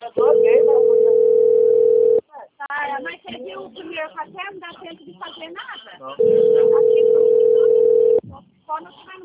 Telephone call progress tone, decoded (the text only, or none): ringback tone